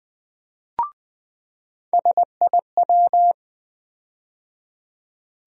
Morse code, heard as a keyed tone, about 20 wpm, 700 Hz. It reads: SIW
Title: USA Call Sign Postfixes - 3 Letters 20wpm